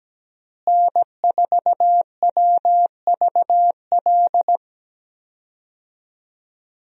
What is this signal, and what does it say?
Morse code, heard as a keyed tone, about 17 wpm, 700 Hz. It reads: N4WVL
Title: USA Call Signs - 5 Characters - Repeated 2x - 17wpm